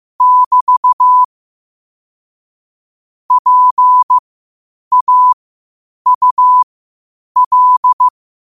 Morse code, heard as a keyed tone, about 15 wpm, spaced as for 8 wpm, 1000 Hz. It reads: <BT> PAUL
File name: qotdfiles / 8wpm